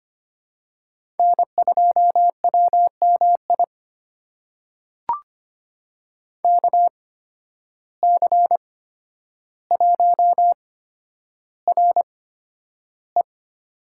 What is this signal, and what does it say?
Morse code, heard as a keyed tone, about 25 wpm, spaced as for 11 wpm, 700 Hz. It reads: N2WMI KC1RE
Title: USA Call Signs - 5 Characters - 8x Character Spacing - 25wpm